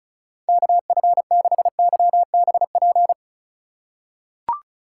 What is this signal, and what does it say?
Morse code, heard as a keyed tone, about 34 wpm, 690 Hz. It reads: KF6YBP
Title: USA Call Signs - 6 Characters - 2x Character Spacing - 35wpm